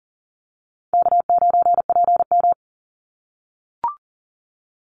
Morse code, heard as a keyed tone, about 40 wpm, 700 Hz. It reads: K9PM
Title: NA SOTA Activator Callsigns 40wpm